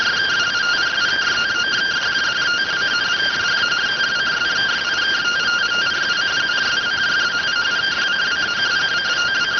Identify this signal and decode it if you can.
Two-tone teleprinter signal, about 170 Hz shift.